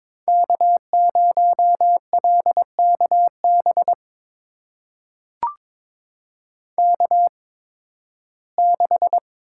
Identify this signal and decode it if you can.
Morse code, heard as a keyed tone, about 22 wpm, spaced as for 14 wpm, 700 Hz. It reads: K0LKB K6
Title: USA Call Signs - 8x Character Spacing - 22wpm